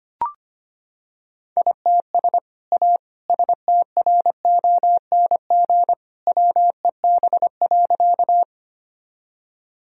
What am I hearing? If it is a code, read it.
Morse code, heard as a keyed tone, about 25 wpm, 690 Hz. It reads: ITS A STRONG WEB.
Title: Sentences from Top 1000 Words - Review Part 4 - 25wpm